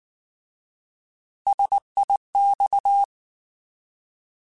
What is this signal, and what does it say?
Morse code, heard as a keyed tone, about 19 wpm, 800 Hz. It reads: SIX
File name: Alphabetical / Alphabetical mp3s / Ss @18